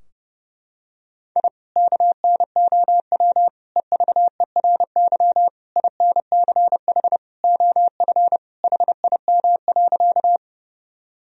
Morse code, heard as a keyed tone, about 30 wpm, 700 Hz. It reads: I KNOW EVERY INCH OF HIM.
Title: Sentences from Top 1000 Words - Review Part 3 - 30wpm